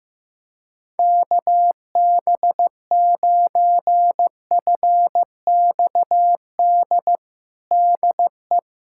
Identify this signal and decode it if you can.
Morse code, heard as a keyed tone, about 15 wpm, 700 Hz. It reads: KB9FXD DE
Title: USA Callsign DE Callsign 15wpm